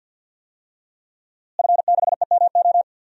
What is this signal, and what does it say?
Morse code, heard as a keyed tone, about 50 wpm, 700 Hz. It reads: F6EMO
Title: DXCC-100 Call Signs - 5 Characters - Repeated 3x - 50wpm